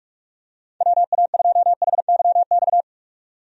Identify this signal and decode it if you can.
Morse code, heard as a keyed tone, about 46 wpm, 700 Hz. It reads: WA2HYX
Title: USA Call Signs - 6 Characters - 45wpm